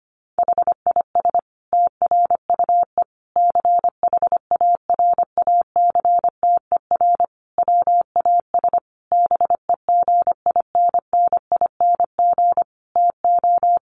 Morse code, heard as a keyed tone, about 25 wpm, 700 Hz. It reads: HIS TRUE CHARACTER WAS BEGINNING TO
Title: Sentences from Top 1200 Words 25wpm